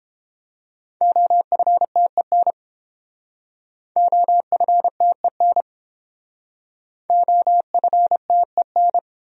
Morse code, heard as a keyed tone, about 29 wpm, 700 Hz. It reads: OFTEN OFTEN OFTEN